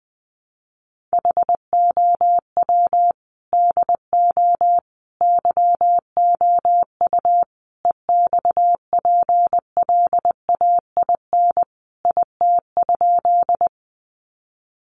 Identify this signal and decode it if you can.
Morse code, heard as a keyed tone, about 20 wpm, 700 Hz. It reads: HOW DO YOU EXPLAIN IT?
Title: Sentences from Top 1100 Words 20wpm